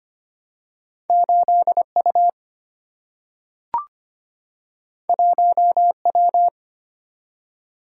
Morse code, heard as a keyed tone, about 25 wpm, 700 Hz. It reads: 8U 1W
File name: Two Letter-Number Combinations - 25wpm